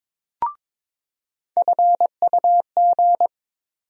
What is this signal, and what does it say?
Morse code, heard as a keyed tone, about 22 wpm, 700 Hz. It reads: FUG
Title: USA Call Sign Postfixes - 3 Letters 22wpm